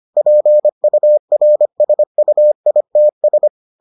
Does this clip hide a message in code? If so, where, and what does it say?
Yes: Morse code from 0.2 to 3.5 s, heard as a keyed tone, about 25 wpm, 600 Hz. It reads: PURSUITS